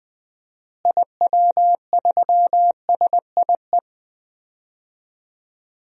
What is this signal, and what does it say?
Morse code, heard as a keyed tone, about 20 wpm, 700 Hz. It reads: IW3SIE
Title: DXCC-100 Call Signs - 6 Characters - Repeated 3x - 20wpm